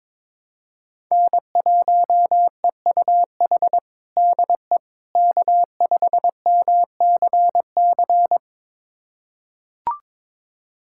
Morse code, heard as a keyed tone, about 22 wpm, 700 Hz. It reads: N1EUH DE K5MCC
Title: USA Callsign DE Callsign - 5 Characters - 22wpm